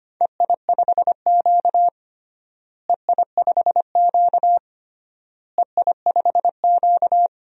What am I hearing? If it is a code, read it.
Morse code, heard as a keyed tone, about 25 wpm, 700 Hz. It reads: EI5Q EI5Q EI5Q
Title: DXCC-100 Call Signs - 4 Characters - Repeated 3x - 25wpm